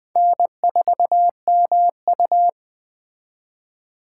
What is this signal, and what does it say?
Morse code, heard as a keyed tone, about 20 wpm, 700 Hz. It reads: N4MU